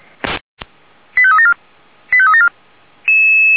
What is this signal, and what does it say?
Five-tone selective call (ZVEI), heard as quick runs of five tones, tones of about 70 ms. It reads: 85374 85374